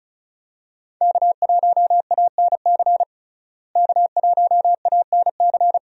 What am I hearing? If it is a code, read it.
Morse code, heard as a keyed tone, about 36 wpm, 700 Hz. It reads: K1ANC K1ANC
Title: USA Call Signs - 5 Characters - Repeated 2x - 35wpm